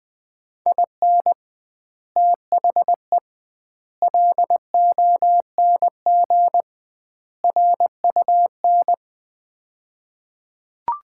Morse code, heard as a keyed tone, about 21 wpm, 700 Hz. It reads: IN THE LONG RUN
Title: 2x Word-Spacing - Sets of 4 Words - 20wpm